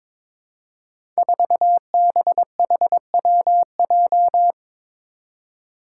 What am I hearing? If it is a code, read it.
Morse code, heard as a keyed tone, about 22 wpm, 700 Hz. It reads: 4BHWJ
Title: Five Challenging Letter-Number Combinations 22wpm